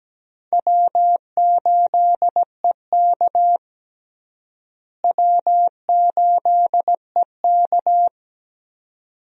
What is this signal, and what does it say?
Morse code, heard as a keyed tone, about 17 wpm, 700 Hz. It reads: W8EK W8EK
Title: USA Call Signs - 4 Characters - Repeated 2x - 17wpm